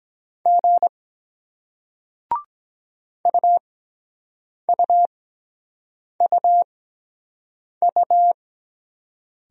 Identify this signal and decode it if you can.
Morse code, heard as a keyed tone, about 23 wpm, spaced as for 7 wpm, 700 Hz. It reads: G UUUU